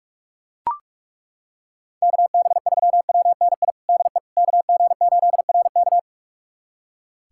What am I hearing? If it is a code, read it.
Morse code, heard as a keyed tone, about 42 wpm, 700 Hz. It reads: KB3WNI DE KG8RK
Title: USA Callsign DE Callsign 45wpm